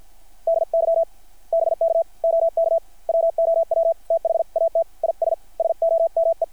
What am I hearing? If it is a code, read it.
Morse code, heard as a keyed tone, about 50 wpm, 660 Hz. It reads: DY BK OK WOW THAT IS SOME
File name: archived / CRSnet